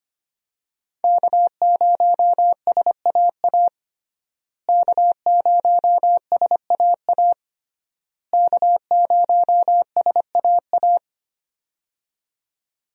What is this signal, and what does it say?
Morse code, heard as a keyed tone, about 25 wpm, 700 Hz. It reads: K0SAA K0SAA K0SAA
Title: USA Call Signs - 5 Characters - Repeated 3x - 25wpm